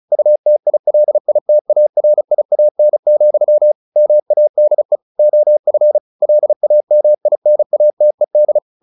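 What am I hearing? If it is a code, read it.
Morse code, heard as a keyed tone, about 34 wpm, 600 Hz. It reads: UTILITARIAN, MADE OF LAMINATED
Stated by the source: Morse practice at 35 wpm from CW med Gnister, sending Ken Follett's novel Never in English.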